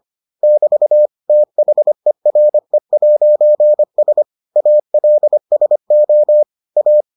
Morse code, heard as a keyed tone, about 25 wpm, 600 Hz. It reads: <BT> THERE'S ALSO A